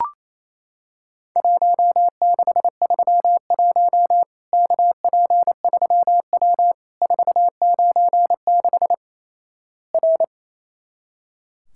Morse code, heard as a keyed tone, about 28 wpm, 700 Hz. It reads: 1631 KP3W 496 R
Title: Worked All Europe DX Contest - QTC Exchange - 28wpm